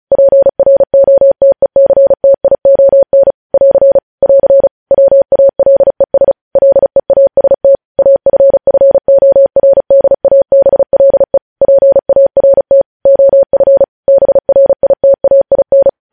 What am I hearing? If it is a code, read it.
Morse code, heard as a keyed tone, about 36 wpm, 550 Hz. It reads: PROTECTION <AR> <AR> WALES LEAST AFFORDABLE PART OF BRITAIN
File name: podcast / News headlines-1x-35x35